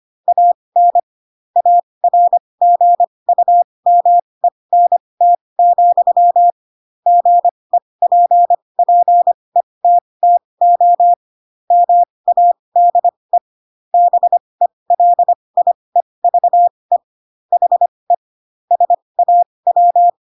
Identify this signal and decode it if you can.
Morse code, heard as a keyed tone, about 25 wpm, spaced as for 20 wpm, 700 Hz. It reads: AN ARGUMENT, GEPPETTO MADE BELIEVE HE SAW
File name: Pinocchio / Pinocchio ch3 15 25 ik1hge